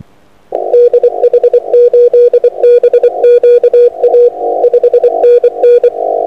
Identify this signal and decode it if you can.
Morse code, heard as a keyed tone, about 24 wpm, 480 Hz. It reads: DH8BQA 5NN
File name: a45xr-6m